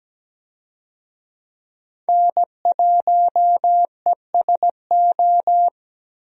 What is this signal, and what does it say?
Morse code, heard as a keyed tone, about 17 wpm, 700 Hz. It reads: N1ESO
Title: USA Call Signs - 5 Characters - 17wpm